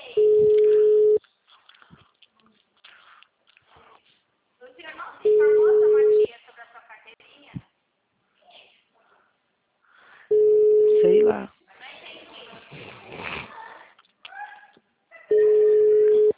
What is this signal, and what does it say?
Ringback tone (Europe, 425 Hz), with speech.